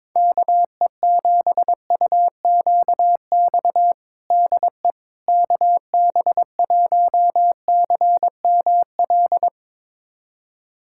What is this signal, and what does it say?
Morse code, heard as a keyed tone, about 22 wpm, 700 Hz. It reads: KE7UQX DE KB1CML
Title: USA Callsign DE Callsign 22wpm